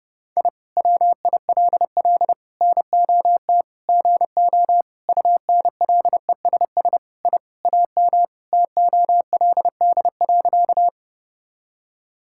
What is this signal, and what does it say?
Morse code, heard as a keyed tone, about 32 wpm, 700 Hz. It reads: I WILL NOT GO UNLESS I AM TOLD.
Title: Sentences from Top 1000 Words - Review Part 6 - 30wpm